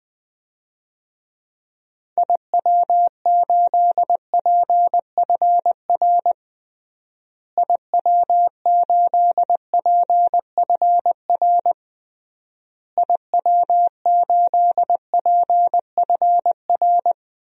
Morse code, heard as a keyed tone, about 21 wpm, 700 Hz. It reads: IW8PFR IW8PFR IW8PFR
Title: DXCC-100 Call Signs - 6 Characters - Repeated 3x - 20wpm